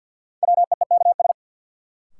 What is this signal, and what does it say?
Morse code, heard as a keyed tone, about 52 wpm, 700 Hz. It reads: WEEKS
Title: Sets of 5 Words - New Word Drill - 50wpm